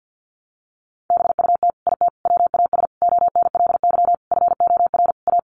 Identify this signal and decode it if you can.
Morse code, heard as a keyed tone, about 50 wpm, 700 Hz. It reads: BUT IT WAS ONLY FOR A
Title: Sentences from Top 1000 Words 50wpm